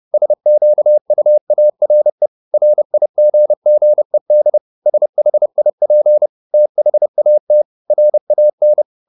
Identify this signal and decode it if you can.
Morse code, heard as a keyed tone, about 30 wpm, 600 Hz. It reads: SQUARE RIGGED SHIP THAT RAN
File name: the old man and the sea / the old man and the sea 1 30wpm